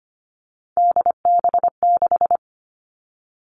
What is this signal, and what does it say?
Morse code, heard as a keyed tone, about 25 wpm, 700 Hz. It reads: DB6
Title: All Cut Numbers - 3-Digit 25wpm